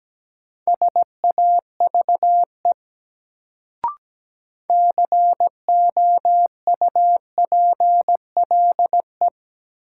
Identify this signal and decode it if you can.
Morse code, heard as a keyed tone, about 17 wpm, 700 Hz. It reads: SAVE COUPLE
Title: Sentences from Top 1100 Words - New Word Drill - 17wpm